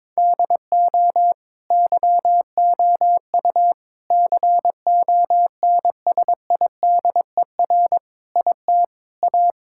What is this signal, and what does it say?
Morse code, heard as a keyed tone, about 22 wpm, 700 Hz. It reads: DO YOU CONSIDER IT A
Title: Sentences from Top 1000 Words - Review Part 6 - 22wpm